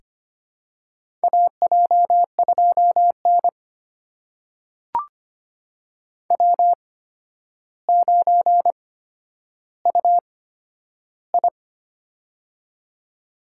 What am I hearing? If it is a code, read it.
Morse code, heard as a keyed tone, about 25 wpm, spaced as for 11 wpm, 700 Hz. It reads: AJ2N W9UI
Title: USA Call Signs - 4 Characters - 8x Character Spacing - 25wpm